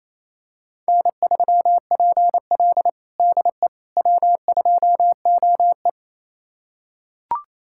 Morse code, heard as a keyed tone, about 27 wpm, 700 Hz. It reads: N3PL DE W2OE
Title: USA Callsign DE Callsign - 4 Characters - 28wpm